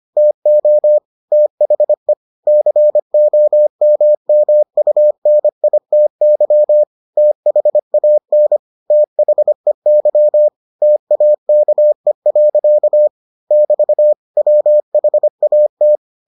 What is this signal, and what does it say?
Morse code, heard as a keyed tone, about 25 wpm, 600 Hz. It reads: TO THE COMMUNITY THAN THEY TAKE. <BT> WHAT